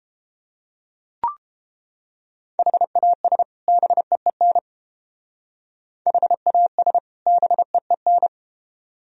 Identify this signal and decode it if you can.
Morse code, heard as a keyed tone, about 29 wpm, 700 Hz. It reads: HAS BEEN HAS BEEN